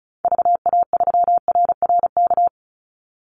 Morse code, heard as a keyed tone, about 34 wpm, 700 Hz. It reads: VA3RRK
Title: Call Signs - 35wpm